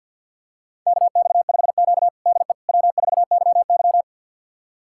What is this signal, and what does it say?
Morse code, heard as a keyed tone, about 53 wpm, 690 Hz. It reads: KX5Q DE W4YY
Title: USA Callsign DE Callsign - 4 Characters - 50wpm